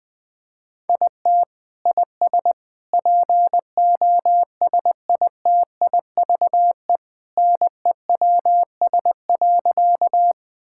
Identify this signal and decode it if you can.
Morse code, heard as a keyed tone, about 21 wpm, 700 Hz. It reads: IT IS POSITIVE NEWS.
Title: Sentences from Top 1000 Words - Review Part 5 - 20wpm